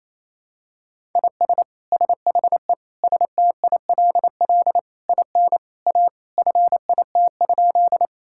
Morse code, heard as a keyed tone, about 28 wpm, 700 Hz. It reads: IS SHE STILL IN A FIT?
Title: Sentences from Top 1000 Words - Review Part 5 - 28wpm